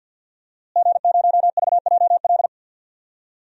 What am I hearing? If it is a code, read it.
Morse code, heard as a keyed tone, about 50 wpm, 700 Hz. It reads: G0VJL
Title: DXCC-100 Call Signs - 5 Characters - Repeated 3x - 50wpm